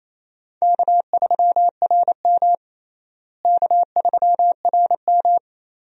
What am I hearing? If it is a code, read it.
Morse code, heard as a keyed tone, about 28 wpm, 700 Hz. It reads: K3RM K3RM
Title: USA Call Signs - 4 Characters - Repeated 2x - 28wpm